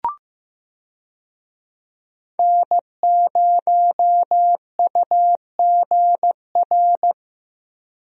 Morse code, heard as a keyed tone, about 15 wpm, 700 Hz. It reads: N0UGR
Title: USA Call Signs - 5 Characters - Repeated 3x - 15wpm